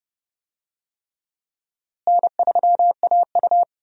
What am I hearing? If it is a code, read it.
Morse code, heard as a keyed tone, about 28 wpm, 700 Hz. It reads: N3AU
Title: USA Call Signs - 4 Characters - Repeated 3x - 30wpm